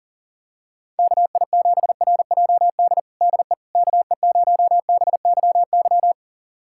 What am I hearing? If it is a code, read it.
Morse code, heard as a keyed tone, about 40 wpm, 700 Hz. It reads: KI7RJD DE KE0BYY
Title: USA Callsign DE Callsign - 6 Characters - 40wpm